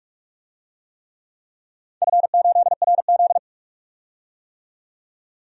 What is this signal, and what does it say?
Morse code, heard as a keyed tone, about 47 wpm, 700 Hz. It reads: F8RZ
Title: DXCC-100 Call Signs - 4 Characters - 45wpm